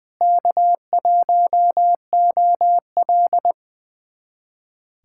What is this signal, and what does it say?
Morse code, heard as a keyed tone, about 21 wpm, 700 Hz. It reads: K1OL
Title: USA Callsign DE Callsign - 4 Characters - 20wpm